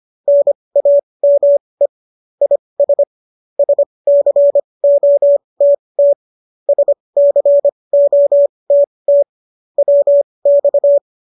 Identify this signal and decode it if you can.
Morse code, heard as a keyed tone, about 25 wpm, spaced as for 20 wpm, 570 Hz. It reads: NAME IS SCOTT SCOTT WX
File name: qso223 20